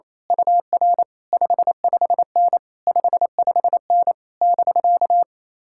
Morse code, heard as a keyed tone, about 26 wpm, 700 Hz. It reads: UR 55N 55N <BK>
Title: Speed-Racing Common QSO Phrases - 28wpm